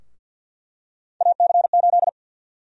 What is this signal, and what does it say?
Morse code, heard as a keyed tone, about 49 wpm, 700 Hz. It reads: AC8
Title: USA Call Sign Prefixes - Repeated 2x - 50wpm